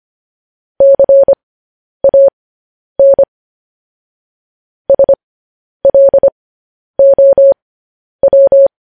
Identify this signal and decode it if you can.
Morse code, heard as a keyed tone, about 25 wpm, spaced as for 10 wpm, 550 Hz. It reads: CAN SLOW
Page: news headlines in morse code at 10 wpm